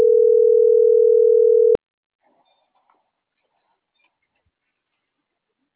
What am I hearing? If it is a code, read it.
Ringback tone (North America), 440 + 480 Hz.